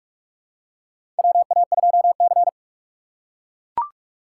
Morse code, heard as a keyed tone, about 42 wpm, 700 Hz. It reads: WA2C
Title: USA Call Signs - 4 Characters - 2x Character Spacing - 45wpm